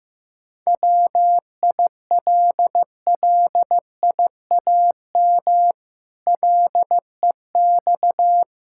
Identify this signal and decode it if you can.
Morse code, heard as a keyed tone, about 15 wpm, 700 Hz. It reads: WILLIAM LEX
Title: Pangram Challenge v2 - 15wpm